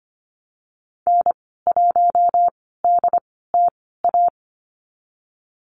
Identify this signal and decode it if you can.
Morse code, heard as a keyed tone, about 25 wpm, spaced as for 16 wpm, 700 Hz. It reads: N1DTA
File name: Lesson 15g - Number 5 - Call Signs 25wpm